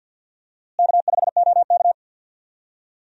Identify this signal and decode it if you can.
Morse code, heard as a keyed tone, about 48 wpm, 700 Hz. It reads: K5OK
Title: USA Call Signs - 4 Characters - Repeated 2x - 50wpm